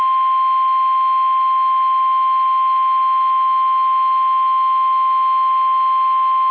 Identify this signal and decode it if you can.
Emergency Alert System (SAME) tones (the 1050 Hz attention tone). It is heard from 0.9 to 6.0 s.
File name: eas / KUSO / KUSO